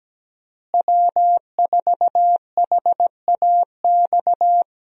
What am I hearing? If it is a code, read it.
Morse code, heard as a keyed tone, about 17 wpm, 700 Hz. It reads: W4HAX
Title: USA Call Signs - 5 Characters - Repeated 3x - 17wpm